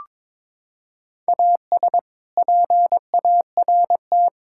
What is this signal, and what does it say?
Morse code, heard as a keyed tone, about 21 wpm, 700 Hz. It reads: AS PART
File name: practice / 22wpm / Sentences with 3-Word Phrases - 22wpm